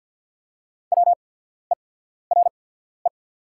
Morse code, heard as a keyed tone, about 50 wpm, spaced as for 14 wpm, 700 Hz. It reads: WERE
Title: 8x Character Spacing - Top 100 Words - 50wpm